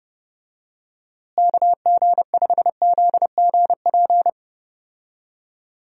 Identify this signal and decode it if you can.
Morse code, heard as a keyed tone, about 32 wpm, 700 Hz. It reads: KG5ZGP
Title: USA Call Signs - 6 Characters - Repeated 3x - 30wpm